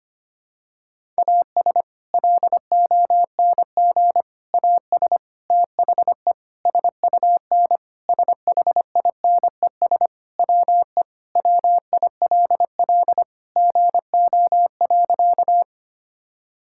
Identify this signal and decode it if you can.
Morse code, heard as a keyed tone, about 25 wpm, 700 Hz. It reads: AS LONG AS THE SUN SHINES WE WILL GO.
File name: practice / 25wpm / Sentences with 4-Word Phrases - 25wpm